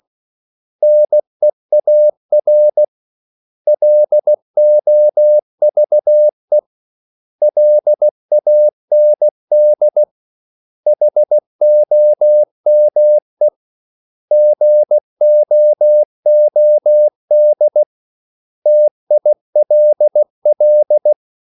Morse code, heard as a keyed tone, about 16 wpm, 600 Hz. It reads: NEAR LOVE LAND HOME GOOD TILL
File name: CW / Practice / 500MostCommonEnglishWords16